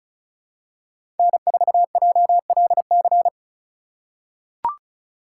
Morse code, heard as a keyed tone, about 36 wpm, 700 Hz. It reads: N4JLC
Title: USA Call Signs - 5 Characters - 4x Character Spacing - 35wpm